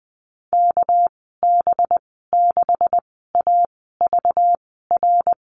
Morse code, heard as a keyed tone, about 20 wpm, spaced as for 15 wpm, 700 Hz. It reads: KB6AVR
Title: 219 Lesson 35g - Number 0 - Call Signs 20wpm